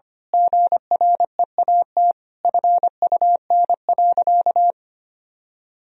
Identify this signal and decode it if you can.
Morse code, heard as a keyed tone, about 25 wpm, 700 Hz. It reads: GREAT FUN.
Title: Sentences from Top 1000 Words - Review Part 4 - 25wpm